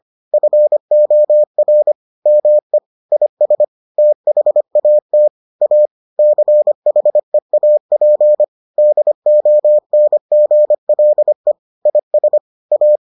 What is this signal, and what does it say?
Morse code, heard as a keyed tone, about 25 wpm, 600 Hz. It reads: FOR ME IS THAT A CHEAP DONGLE IS A